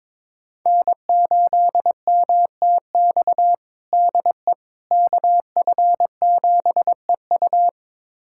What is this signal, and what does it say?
Morse code, heard as a keyed tone, about 22 wpm, 700 Hz. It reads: N8MTX DE KF7EU
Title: USA Callsign DE Callsign - 5 Characters - 22wpm